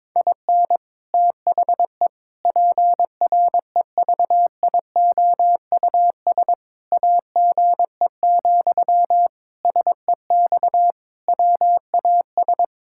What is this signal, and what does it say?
Morse code, heard as a keyed tone, about 21 wpm, 700 Hz. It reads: IN THE PREVIOUS AGE, SEX WAS
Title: Sentences from Top 1600 Words 22wpm